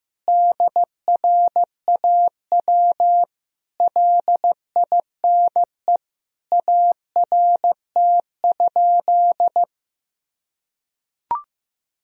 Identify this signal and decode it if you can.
Morse code, heard as a keyed tone, about 15 wpm, 700 Hz. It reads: DRAW LINE ART?
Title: Sentences from Top 1000 Words - Review Part 3 - 15wpm